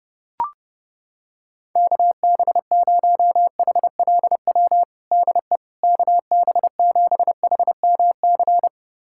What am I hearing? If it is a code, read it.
Morse code, heard as a keyed tone, about 32 wpm, 700 Hz. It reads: KB0HLW DE KB7HMC
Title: USA Callsign DE Callsign - 6 Characters - 30wpm